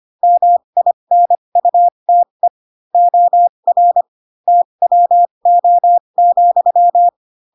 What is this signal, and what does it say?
Morse code, heard as a keyed tone, about 25 wpm, spaced as for 22 wpm, 700 Hz. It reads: MINUTE OR TWO,
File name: Pinocchio / Pinocchio ch6 18 25 ik1hge